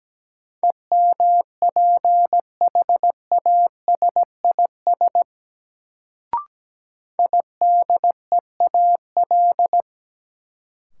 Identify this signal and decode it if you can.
Morse code, heard as a keyed tone, about 17 wpm, 700 Hz. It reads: EMPHASIS IDEAL
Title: Sentences from Top 1400 Words - New Word Drill - 17wpm